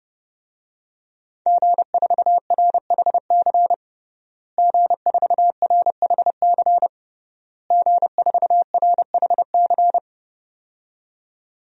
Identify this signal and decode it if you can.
Morse code, heard as a keyed tone, about 30 wpm, 700 Hz. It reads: G4RHC G4RHC G4RHC
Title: DXCC-100 Call Signs - 5 Characters - Repeated 3x - 30wpm